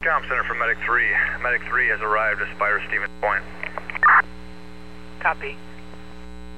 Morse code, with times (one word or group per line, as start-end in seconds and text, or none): none